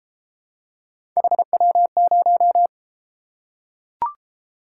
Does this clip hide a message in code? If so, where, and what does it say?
Yes: Morse code from 1.2 to 2.7 s, heard as a keyed tone, about 33 wpm, 700 Hz. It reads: HW0